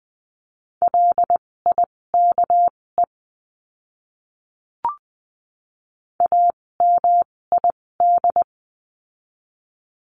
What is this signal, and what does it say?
Morse code, heard as a keyed tone, about 20 wpm, spaced as for 16 wpm, 700 Hz. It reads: LIKE AMID
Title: 256 Lesson 42a - Words 12-20wpm